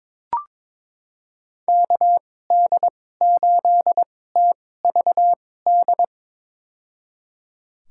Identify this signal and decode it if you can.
Morse code, heard as a keyed tone, about 22 wpm, spaced as for 16 wpm, 700 Hz. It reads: KD8TVD
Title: USA Call Signs - 6 Characters - 2x Character Spacing - 22wpm